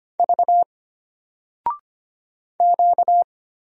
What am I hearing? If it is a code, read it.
Morse code, heard as a keyed tone, about 25 wpm, 700 Hz. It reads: V Q
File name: Single-Letter-Number-25wpm